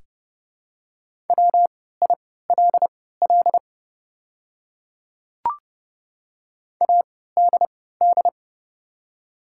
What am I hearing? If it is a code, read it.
Morse code, heard as a keyed tone, about 30 wpm, spaced as for 17 wpm, 700 Hz. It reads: WILL ADD
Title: 3x Character Spacing - Top 200 Words - 30wpm